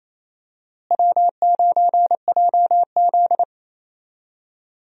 Morse code, heard as a keyed tone, about 28 wpm, 700 Hz. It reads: W9JZ
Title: USA Call Signs - 4 Characters - Repeated 2x - 28wpm